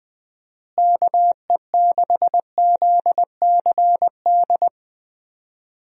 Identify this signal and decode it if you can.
Morse code, heard as a keyed tone, about 20 wpm, 700 Hz. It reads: KE6ZCD
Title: USA Call Signs - 6 Characters - Repeated 2x - 20wpm